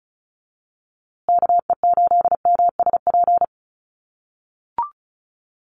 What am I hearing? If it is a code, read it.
Morse code, heard as a keyed tone, about 34 wpm, 700 Hz. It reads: KE8MSP
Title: USA Call Signs Repeated 2x 35wpm